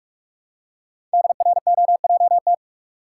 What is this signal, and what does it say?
Morse code, heard as a keyed tone, about 44 wpm, 700 Hz. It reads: DROJT
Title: Code Groups - 5 Letters - 45wpm